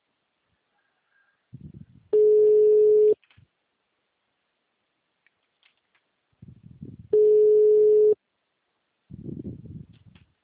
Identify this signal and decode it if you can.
Ringback tone (Europe, 425 Hz).